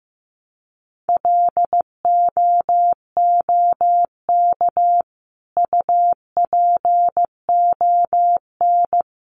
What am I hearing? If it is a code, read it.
Morse code, heard as a keyed tone, about 15 wpm, 700 Hz. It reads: LOOK UPON
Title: Sentences from Top 1100 Words 15wpm